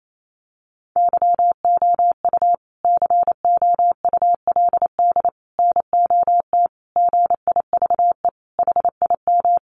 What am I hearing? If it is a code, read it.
Morse code, heard as a keyed tone, about 28 wpm, 700 Hz. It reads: YOU COULD NOT GIVE HIM
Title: Sentences from Top 1500 Words 28wpm